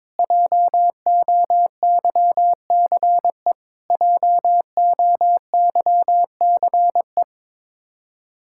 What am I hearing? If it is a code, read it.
Morse code, heard as a keyed tone, about 21 wpm, 700 Hz. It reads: JOYCE JOYCE
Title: Practice QSO - Exchange 1 - Beginning - 22wpm